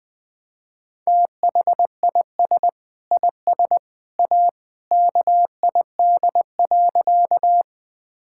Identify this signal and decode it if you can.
Morse code, heard as a keyed tone, about 20 wpm, 700 Hz. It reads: THIS IS A KID.
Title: Sentences from Top 1600 Words 20wpm